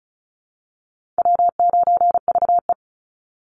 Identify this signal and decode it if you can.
Morse code, heard as a keyed tone, about 34 wpm, 700 Hz. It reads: W9VE
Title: Easy Call Signs Repeated 2x - 35wpm